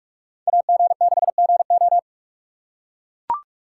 Morse code, heard as a keyed tone, about 44 wpm, 700 Hz. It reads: AG6GO